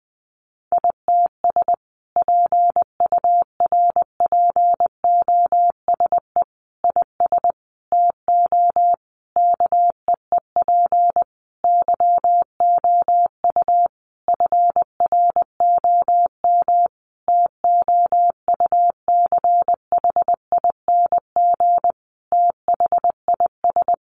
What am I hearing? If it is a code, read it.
Morse code, heard as a keyed tone, about 19 wpm, 700 Hz. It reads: ITS PURPOSE IS TO KEEP YOU FROM TOUCHING THIS
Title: A Sound Of Thunder 20wpm